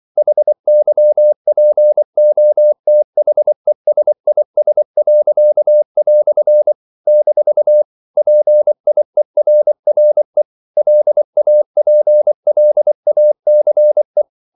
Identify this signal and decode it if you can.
Morse code, heard as a keyed tone, about 24 wpm, 600 Hz. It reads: HYPOTHESIS." - PIERRE LAPLACE
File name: CW / Practice / EnglishQuotations0424